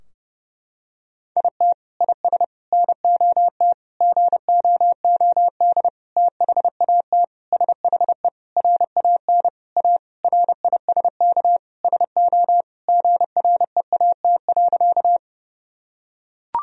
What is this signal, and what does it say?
Morse code, heard as a keyed tone, about 32 wpm, 700 Hz. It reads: IT IS NOT GOOD THAT SHE RAN A RISK SO GREAT.
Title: Sentences from Top 1000 Words - Review Part 4 - 30wpm